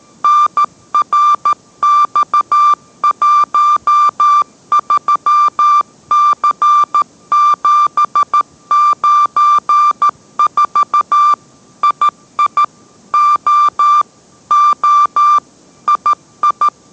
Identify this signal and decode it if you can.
Morse code, heard as a keyed tone, about 13 wpm, 1180 Hz. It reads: NRX13C794 II O O II